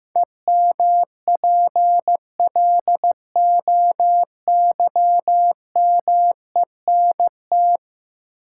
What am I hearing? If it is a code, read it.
Morse code, heard as a keyed tone, about 15 wpm, 700 Hz. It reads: EMPLOYMENT
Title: Sentences from Top 1500 Words - New Word Drill - 15wpm